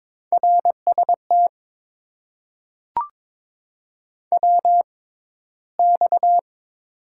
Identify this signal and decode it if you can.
Morse code, heard as a keyed tone, about 22 wpm, spaced as for 14 wpm, 700 Hz. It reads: RST WX
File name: Top 90 QSO Elements - 6x Character Spacing - 22wpm